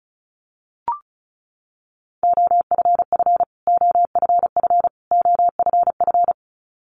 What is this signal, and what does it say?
Morse code, heard as a keyed tone, about 34 wpm, 700 Hz. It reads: OFF OFF OFF
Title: Top 500 Words Repeated 35wpm